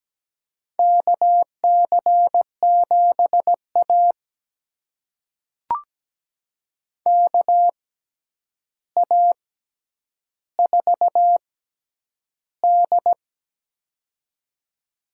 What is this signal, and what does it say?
Morse code, heard as a keyed tone, about 17 wpm, spaced as for 9 wpm, 700 Hz. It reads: KC7A KA4D
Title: USA Call Signs - 4 Characters - 6x Character Spacing - 17wpm